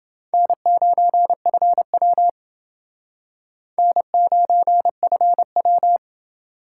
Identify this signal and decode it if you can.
Morse code, heard as a keyed tone, about 29 wpm, 700 Hz. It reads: N9FW N9FW